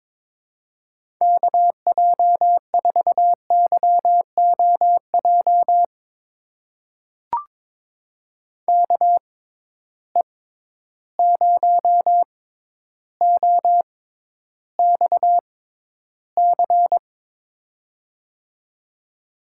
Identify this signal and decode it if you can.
Morse code, heard as a keyed tone, about 21 wpm, spaced as for 11 wpm, 700 Hz. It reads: KJ4YOJ KE0OXC
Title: USA Call Signs - 6 Characters - 6x Character Spacing - 22wpm